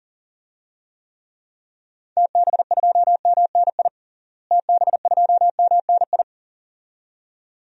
Morse code, heard as a keyed tone, about 40 wpm, 700 Hz. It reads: TB2MNI TB2MNI
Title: DXCC-100 Call Signs - 6 Characters - Repeated 2x - 40wpm